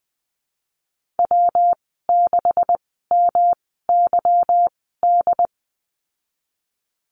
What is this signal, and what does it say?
Morse code, heard as a keyed tone, about 20 wpm, spaced as for 15 wpm, 700 Hz. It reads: W6MYD
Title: 179 Lesson 29g - Slash - Call Signs 20wpm